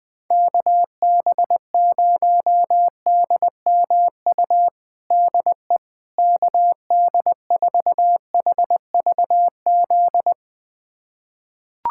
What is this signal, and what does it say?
Morse code, heard as a keyed tone, about 20 wpm, 700 Hz. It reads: KB0DMU DE KD4HVZ
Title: USA Callsign DE Callsign 20wpm